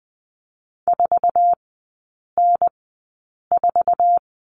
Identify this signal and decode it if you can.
Morse code, heard as a keyed tone, about 20 wpm, spaced as for 8 wpm, 700 Hz. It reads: 4N4
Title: 047 Lesson 9e - Number 4 - Three Characters 20wpm